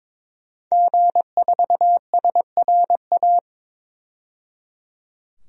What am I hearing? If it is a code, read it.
Morse code, heard as a keyed tone, about 21 wpm, 700 Hz. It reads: G4SRA
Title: DXCC-100 Call Signs - 5 Characters - 22wpm